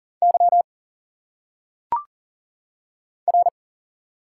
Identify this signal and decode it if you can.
Morse code, heard as a keyed tone, about 40 wpm, 700 Hz. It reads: Y R
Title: Single Letters - Phonetic Pronunciation 40wpm